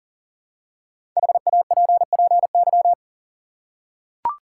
Morse code, heard as a keyed tone, about 40 wpm, 700 Hz. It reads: HAPPY
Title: Five Letter Words - 40wpm